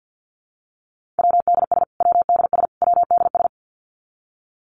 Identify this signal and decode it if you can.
Morse code, heard as a keyed tone, about 50 wpm, 700 Hz. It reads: WDS WDS WDS
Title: Abbreviations Repeated 3x - 50wpm